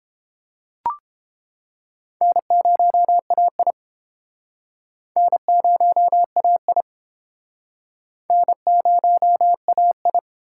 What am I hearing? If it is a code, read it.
Morse code, heard as a keyed tone, about 30 wpm, 700 Hz. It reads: N0AI N0AI N0AI